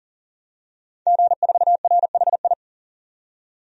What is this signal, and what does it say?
Morse code, heard as a keyed tone, about 40 wpm, 700 Hz. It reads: G4RHI
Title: DXCC-100 Call Signs - 5 Characters - 40wpm